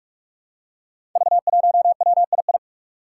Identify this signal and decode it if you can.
Morse code, heard as a keyed tone, about 46 wpm, 700 Hz. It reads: V1WII
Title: Five Challenging Letter-Number Combinations 45wpm